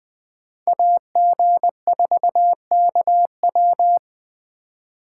Morse code, heard as a keyed tone, about 20 wpm, 700 Hz. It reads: AG4KW
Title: Speed Racing - USA Call Signs 17wpm